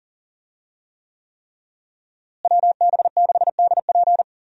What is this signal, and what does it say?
Morse code, heard as a keyed tone, about 40 wpm, 700 Hz. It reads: WB6DP
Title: USA Call Signs - 5 Characters - Repeated 3x - 40wpm